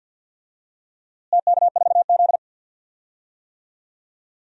Morse code, heard as a keyed tone, about 46 wpm, 700 Hz. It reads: TK4Z
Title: DXCC-100 Call Signs - 4 Characters - Repeated 2x - 50wpm